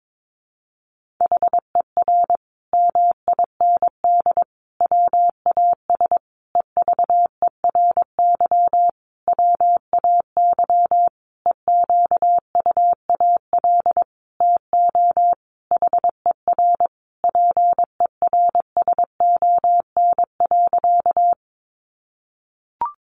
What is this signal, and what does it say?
Morse code, heard as a keyed tone, about 22 wpm, 700 Hz. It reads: HER MIND WAS EVERY WAY EQUAL TO HER PERSON.
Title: Sentences from Top 1400 Words 22wpm